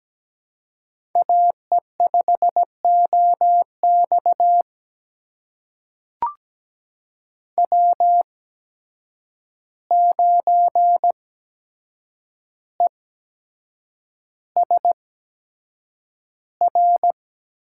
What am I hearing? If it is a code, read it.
Morse code, heard as a keyed tone, about 17 wpm, spaced as for 7 wpm, 700 Hz. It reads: AE5OX W9ESR
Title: USA Call Signs - 5 Characters - 8x Character Spacing - 17wpm